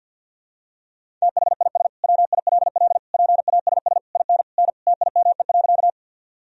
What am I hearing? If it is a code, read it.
Morse code, heard as a keyed tone, about 52 wpm, 700 Hz. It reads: THIS WILL PASS IN N TIME.
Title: Sentences from Top 400 Words and Abbreviations 50wpm